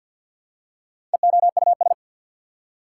Morse code, heard as a keyed tone, about 48 wpm, 700 Hz. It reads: EOUS
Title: Prefixes and Suffixes 50wpm